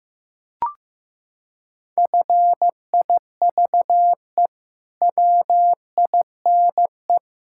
Morse code, heard as a keyed tone, about 15 wpm, 700 Hz. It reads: FIVE WINE
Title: Pangram Challenge v2 - 15wpm